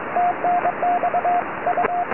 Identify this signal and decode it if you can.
Morse code, heard as a keyed tone, about 22 wpm, 710 Hz, with radio noise. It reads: TNX U
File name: tropo / sp6asd 23cm